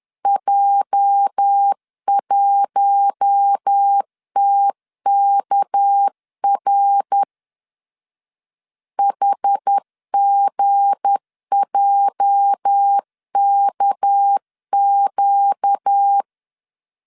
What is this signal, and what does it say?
Morse code, heard as a keyed tone, about 10 wpm, 800 Hz. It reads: J1TKR HGJKQ
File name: LEZIONI / 8 YJ45 / 8 YJ 10wpm 2651